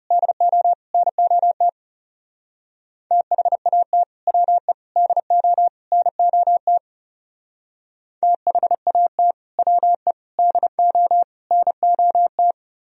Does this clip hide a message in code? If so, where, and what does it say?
Yes: Morse code from 0.1 to 12.5 s, heard as a keyed tone, about 38 wpm, 700 Hz. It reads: DO NOT THAT WE DO NOT THAT WE DO NOT